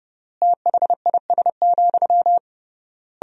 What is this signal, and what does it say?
Morse code, heard as a keyed tone, about 30 wpm, 700 Hz. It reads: THIS,
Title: 3x Word-Spacing - Sentences from Top 400 Words - 30wpm